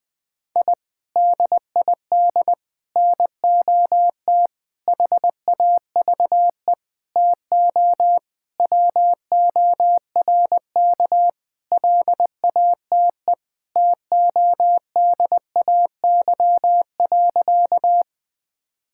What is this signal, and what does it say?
Morse code, heard as a keyed tone, about 20 wpm, 700 Hz. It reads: I DID NOT HAVE TO WORK LATE TODAY.
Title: Sentences with 5-Word Phrases - 20wpm